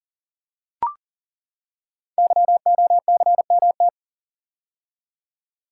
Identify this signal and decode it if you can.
Morse code, heard as a keyed tone, about 42 wpm, 700 Hz. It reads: YOCMT
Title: Code Groups - 5 Letters - 40wpm